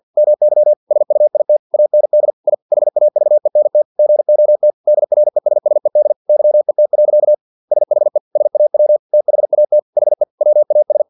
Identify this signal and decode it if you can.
Morse code, heard as a keyed tone, about 50 wpm, 600 Hz. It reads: MY SUIT AND I HAVENT GOT DRESSED YET. SHE SAW THAT HE WAS
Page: Never 49wpm